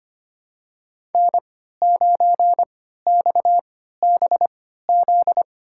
Morse code, heard as a keyed tone, about 25 wpm, spaced as for 14 wpm, 700 Hz. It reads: N9XBZ
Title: USA Call Signs - 3x Character Spacing - 25wpm